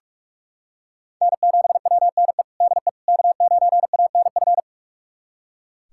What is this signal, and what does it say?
Morse code, heard as a keyed tone, about 43 wpm, 700 Hz. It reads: N7WNE DE K9ANF
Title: USA Callsign DE Callsign - 5 Characters - 45wpm